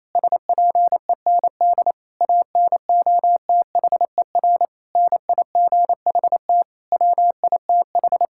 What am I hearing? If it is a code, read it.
Morse code, heard as a keyed tone, about 28 wpm, 700 Hz. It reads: SPEND ANOTHER NIGHT WITH
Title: Sentences from Top 1000 Words - Review Part 5 - 28wpm